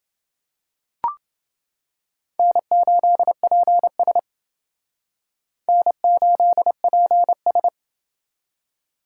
Morse code, heard as a keyed tone, about 29 wpm, 700 Hz. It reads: N8PS N8PS